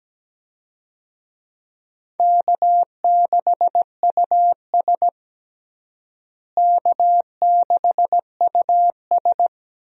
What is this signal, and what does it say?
Morse code, heard as a keyed tone, about 17 wpm, 700 Hz. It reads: K6US K6US
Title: USA Call Signs - 4 Characters - Repeated 2x - 17wpm